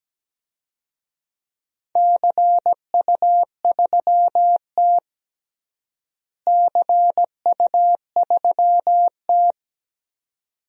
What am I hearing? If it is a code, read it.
Morse code, heard as a keyed tone, about 17 wpm, 700 Hz. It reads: CU3T CU3T
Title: DXCC-100 Call Signs - 4 Characters - Repeated 2x - 17wpm